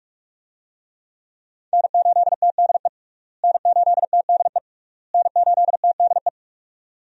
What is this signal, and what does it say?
Morse code, heard as a keyed tone, about 46 wpm, 700 Hz. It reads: N8TDE N8TDE N8TDE
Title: USA Call Signs - 5 Characters - Repeated 3x - 45wpm